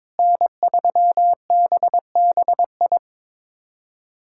Morse code, heard as a keyed tone, about 22 wpm, 700 Hz. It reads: N3BBI
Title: USA Callsign DE Callsign - 5 Characters - 22wpm